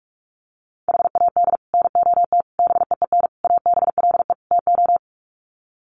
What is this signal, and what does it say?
Morse code, heard as a keyed tone, about 44 wpm, 700 Hz. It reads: HAD NOT BEEN ABLE TO
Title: Sets of 5 Words Encore 45wpm